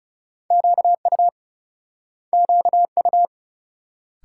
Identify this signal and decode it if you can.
Morse code, heard as a keyed tone, about 41 wpm, 700 Hz. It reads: QU QU